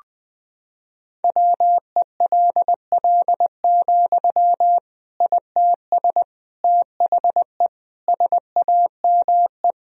Morse code, heard as a keyed tone, about 21 wpm, 700 Hz. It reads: WELL, ITS THE SAME
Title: Sentences from Top 1000 Words - Review Part 4 - 20wpm